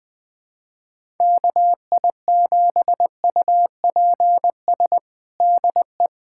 Morse code, heard as a keyed tone, about 21 wpm, 700 Hz. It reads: KI7UPS DE
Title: USA Callsign DE Callsign - 6 Characters - 20wpm